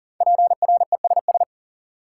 Morse code, heard as a keyed tone, about 40 wpm, 700 Hz. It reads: PRESS
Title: Sentences from Top 800 Words - New Word Drill - 40wpm